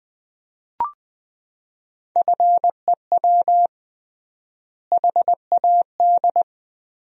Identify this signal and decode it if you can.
Morse code, heard as a keyed tone, about 20 wpm, 700 Hz. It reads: FEW HAD
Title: 3x Word-Spacing - Sentences from Top 300 Words - 20wpm